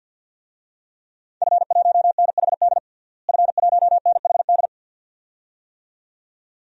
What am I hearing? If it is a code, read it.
Morse code, heard as a keyed tone, about 50 wpm, 700 Hz. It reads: F1NHD F1NHD
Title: DXCC-100 Call Signs - 5 Characters - Repeated 2x - 50wpm